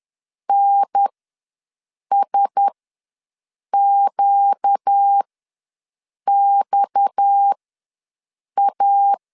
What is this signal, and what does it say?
Morse code, heard as a keyed tone, about 10 wpm, spaced as for 6 wpm, 800 Hz. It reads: NSQXA
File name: LEZIONI / 6 PKX0 / 6 PKX ws 10wpm 3706